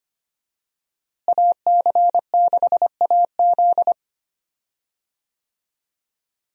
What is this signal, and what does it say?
Morse code, heard as a keyed tone, about 25 wpm, 700 Hz. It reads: AC6AZ
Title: USA Call Signs - 5 Characters - Repeated 3x - 25wpm